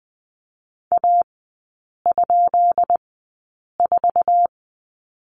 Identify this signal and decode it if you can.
Morse code, heard as a keyed tone, about 20 wpm, spaced as for 8 wpm, 700 Hz. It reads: A?4
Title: 172 Lesson 28e - Number 9 - Three Characters 20wpm